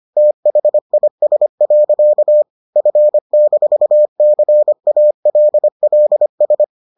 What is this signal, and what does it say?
Morse code, heard as a keyed tone, about 25 wpm, 600 Hz. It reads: THIS. F-CALLS